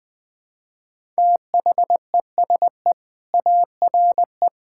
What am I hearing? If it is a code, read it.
Morse code, heard as a keyed tone, about 20 wpm, 700 Hz. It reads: THESE ARE
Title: Sentences with 4-Word Phrases - 20wpm